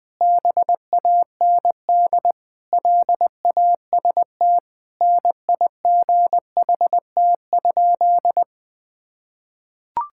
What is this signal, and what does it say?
Morse code, heard as a keyed tone, about 20 wpm, 700 Hz. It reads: BAND LAST NIGHT?
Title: Sentences from Top 1600 Words 20wpm